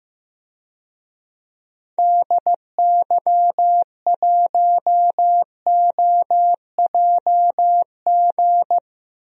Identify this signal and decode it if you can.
Morse code, heard as a keyed tone, about 15 wpm, 700 Hz. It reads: DY1OJG
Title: DXCC-100 Call Signs - 6 Characters - Repeated 2x - 15wpm